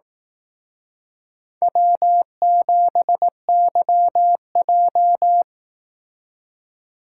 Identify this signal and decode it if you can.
Morse code, heard as a keyed tone, about 18 wpm, 700 Hz. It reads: W7YJ
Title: Speed Racing - USA 1x2 Call Signs 15wpm